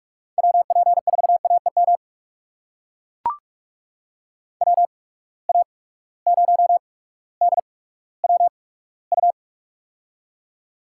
Morse code, heard as a keyed tone, about 43 wpm, spaced as for 19 wpm, 700 Hz. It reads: WP4AEM WA0DWU
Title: USA Call Signs - 6 Characters - 8x Character Spacing - 45wpm